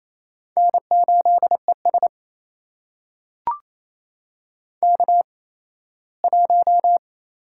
Morse code, heard as a keyed tone, about 28 wpm, spaced as for 17 wpm, 700 Hz. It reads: N8ES K1
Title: USA Call Signs - 4 Characters - 8x Character Spacing - 28wpm